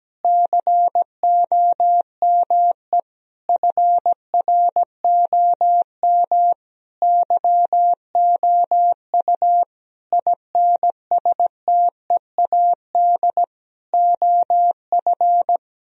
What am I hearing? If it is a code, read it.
Morse code, heard as a keyed tone, about 17 wpm, 700 Hz. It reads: COME FROM YOU INSTEAD OF
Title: Sentences from Top 1000 Words - Review Part 4 - 17wpm